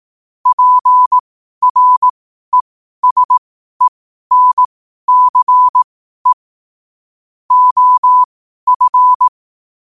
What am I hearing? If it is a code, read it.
Morse code, heard as a keyed tone, about 18 wpm, spaced as for 12 wpm, 1000 Hz. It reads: PRESENCE OF
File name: qotdfiles / 12wpm